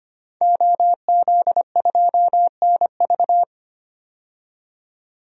Morse code, heard as a keyed tone, about 25 wpm, 700 Hz. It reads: OZ2NV
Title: DXCC-100 Call Signs - 5 Characters - 25wpm